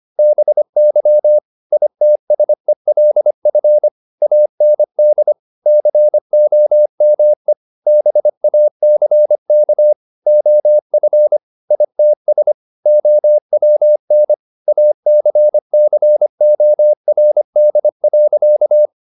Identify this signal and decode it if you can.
Morse code, heard as a keyed tone, about 25 wpm, 600 Hz. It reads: BY ITSELF AND COME BACK OF ITS OWN ACCORD.
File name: the old man and the sea part 1 25wpm